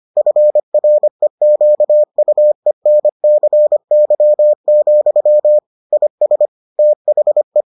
Morse code, heard as a keyed tone, about 25 wpm, 600 Hz. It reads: FREQUENCY, IS THE